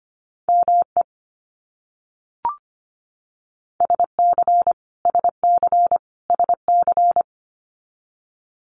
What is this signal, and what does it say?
Morse code, heard as a keyed tone, about 25 wpm, 700 Hz. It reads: ME SC SC SC
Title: US States Repeated 3x 25wpm